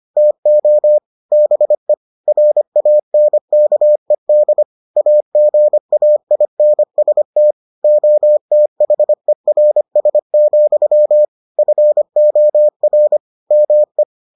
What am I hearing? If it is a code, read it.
Morse code, heard as a keyed tone, about 25 wpm, 600 Hz. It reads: TO BE RANKED AGAINST OTHERS, FOR ME